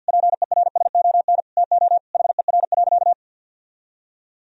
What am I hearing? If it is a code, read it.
Morse code, heard as a keyed tone, about 50 wpm, 700 Hz. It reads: PERSON TO HER.
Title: Sentences from Top 1000 Words - Review Part 3 - 50wpm